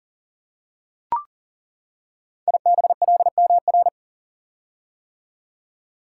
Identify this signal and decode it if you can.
Morse code, heard as a keyed tone, about 40 wpm, 700 Hz. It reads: IBLMR
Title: Code Groups - 5 Letters - 40wpm